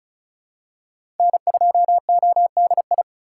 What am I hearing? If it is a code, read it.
Morse code, heard as a keyed tone, about 36 wpm, 700 Hz. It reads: N2ODI
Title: USA Call Signs - 8x Character Spacing - 35wpm